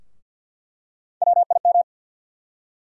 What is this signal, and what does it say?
Morse code, heard as a keyed tone, about 50 wpm, 700 Hz. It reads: WIM
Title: USA Call Sign Postfixes - 3 Letters 50wpm